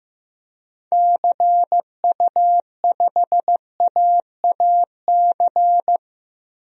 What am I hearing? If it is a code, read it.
Morse code, heard as a keyed tone, about 15 wpm, 700 Hz. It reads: CU5AAC
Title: DXCC-100 Call Signs - 6 Characters - 15wpm